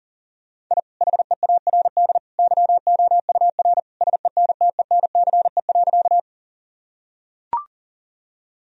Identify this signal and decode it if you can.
Morse code, heard as a keyed tone, about 40 wpm, 700 Hz. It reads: I HEARD YOUR SENTENCE.
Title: Sentences from Top 1000 Words - Review Part 1 - 40wpm